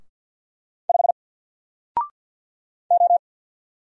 Morse code, heard as a keyed tone, about 48 wpm, 700 Hz. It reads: 5 O
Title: Single Character (Letter:Number:Punctuation:Prosign) Alt 50wpm